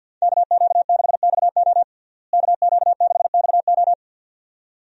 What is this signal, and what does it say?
Morse code, heard as a keyed tone, about 50 wpm, 700 Hz. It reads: KQ6XO KQ6XO
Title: USA Call Signs - 5 Characters - Repeated 2x - 50wpm